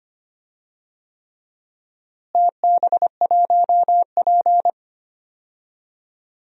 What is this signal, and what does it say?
Morse code, heard as a keyed tone, about 25 wpm, 700 Hz. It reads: TB1P
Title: DXCC-100 Call Signs - 4 Characters - Repeated 3x - 25wpm